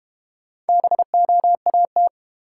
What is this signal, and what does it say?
Morse code, heard as a keyed tone, about 30 wpm, 700 Hz. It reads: BOAT
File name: Easy Words - Speed-Racing with Hints - 25wpm